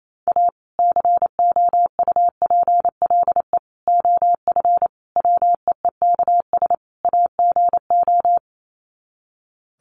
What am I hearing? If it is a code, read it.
Morse code, heard as a keyed tone, about 28 wpm, 700 Hz. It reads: A COUPLE OF WEEKS AGO
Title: Sets of 5 Words 28wpm